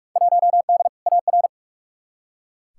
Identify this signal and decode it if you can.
Morse code, heard as a keyed tone, about 43 wpm, 700 Hz. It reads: 1D AR
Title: Field Day Exchanges - 45wpm